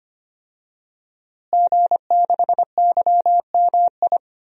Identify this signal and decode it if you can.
Morse code, heard as a keyed tone, about 25 wpm, 700 Hz. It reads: G6YMI
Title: DXCC-100 Call Signs - 5 Characters - Repeated 2x - 25wpm